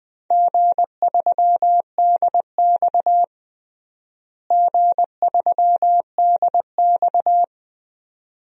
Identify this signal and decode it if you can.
Morse code, heard as a keyed tone, about 21 wpm, 700 Hz. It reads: G3DX G3DX
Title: DXCC-100 Call Signs - 4 Characters - Repeated 2x - 20wpm